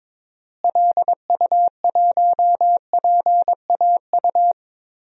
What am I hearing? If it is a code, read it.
Morse code, heard as a keyed tone, about 22 wpm, 700 Hz. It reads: LU1PAU
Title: DXCC-100 Call Signs - 6 Characters - Repeated 2x - 22wpm